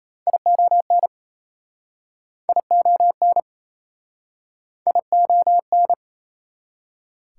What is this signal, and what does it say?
Morse code, heard as a keyed tone, about 37 wpm, 690 Hz. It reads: ION ION ION